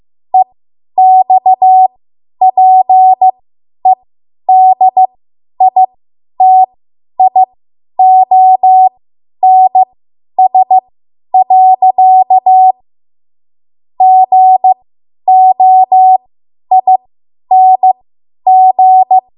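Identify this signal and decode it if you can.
Morse code, heard as a keyed tone, about 15 wpm, spaced as for 10 wpm, 750 Hz. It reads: EXPEDITIONS. GOING
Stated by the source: Morse code practice run at 10 WPM from a code practice archive.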